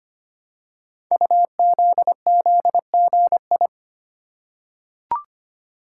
Morse code, heard as a keyed tone, about 25 wpm, 700 Hz. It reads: UZZGI
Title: Code Groups - 5 Letters - 25wpm